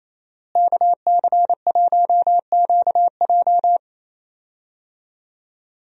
Morse code, heard as a keyed tone, about 28 wpm, 700 Hz. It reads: KC1QJ
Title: USA Call Signs - 5 Characters - Repeated 2x - 28wpm